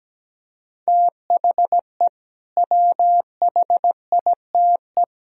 Morse code, heard as a keyed tone, about 17 wpm, 700 Hz. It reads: THE WHITE
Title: Speed-Racing - Sets of 3 Words 17wpm